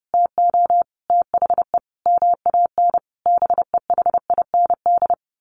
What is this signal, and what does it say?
Morse code, heard as a keyed tone, about 30 wpm, 700 Hz. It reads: TO THE MAN BEHIND
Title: A Sound Of Thunder 30wpm